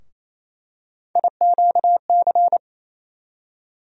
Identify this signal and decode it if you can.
Morse code, heard as a keyed tone, about 28 wpm, 700 Hz. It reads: IQC
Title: USA Call Sign Postfixes - 3 Letters 28wpm